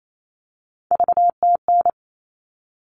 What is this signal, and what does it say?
Morse code, heard as a keyed tone, about 28 wpm, 700 Hz. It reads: VTN
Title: All Cut Numbers - 3-Digit 28wpm